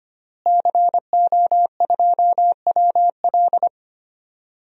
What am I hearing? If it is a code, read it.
Morse code, heard as a keyed tone, about 25 wpm, 700 Hz. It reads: CO2WL
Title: DXCC-100 Call Signs - 5 Characters - Repeated 3x - 25wpm